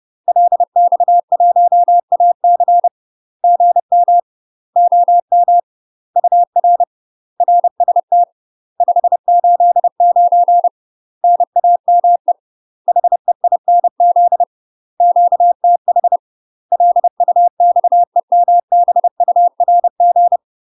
Morse code, heard as a keyed tone, about 30 wpm, 700 Hz. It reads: LX1AC GM OM UR RST 589 NAME HEINZ QTH LUXEMBURG
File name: UK MASTER QSOs TRAINING/30WPM